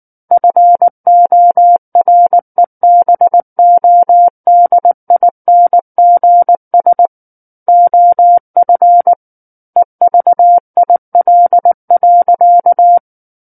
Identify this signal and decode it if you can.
Morse code, heard as a keyed tone, about 19 wpm, 700 Hz. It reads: FOREBODINGS OF EVIL.
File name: Book One / 10-Chapter 10 - 19 WPM